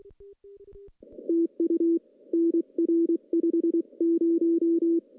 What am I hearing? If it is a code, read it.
Morse code, heard as a keyed tone, about 23 wpm, 330 Hz. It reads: TU NR 50
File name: contests / CW-SS-k5tr- -0000